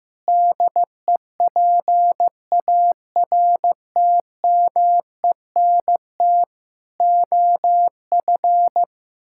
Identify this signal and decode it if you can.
Morse code, heard as a keyed tone, about 16 wpm, 700 Hz. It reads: DEPARTMENT OF
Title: Sentences from Top 1000 Words - Review Part 5 - 15wpm